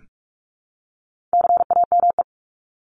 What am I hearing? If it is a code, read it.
Morse code, heard as a keyed tone, about 44 wpm, 700 Hz. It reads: CAME
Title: Top 200 Words Repeated 45wpm